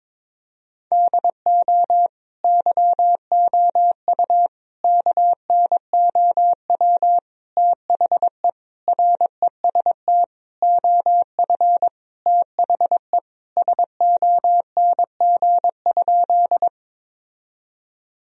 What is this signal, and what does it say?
Morse code, heard as a keyed tone, about 22 wpm, 700 Hz. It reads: DO YOU KNOW THE REST OF THE SONG?
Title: Sentences from Top 1000 Words - Review Part 2 - 22wpm